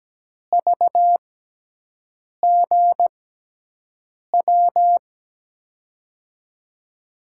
Morse code, heard as a keyed tone, about 17 wpm, spaced as for 6 wpm, 700 Hz. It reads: VGW
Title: USA Call Signs - 6 Characters - 6x Character Spacing - 17wpm